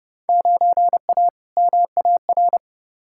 Morse code, heard as a keyed tone, about 30 wpm, 700 Hz. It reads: 9A MAR
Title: Field Day Exchanges Encore - 30wpm